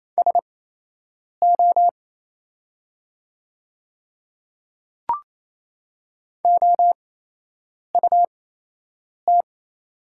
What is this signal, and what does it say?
Morse code, heard as a keyed tone, about 28 wpm, spaced as for 8 wpm, 700 Hz. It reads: SO OUT
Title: 8x Character Spacing - Top 100 Words - 28wpm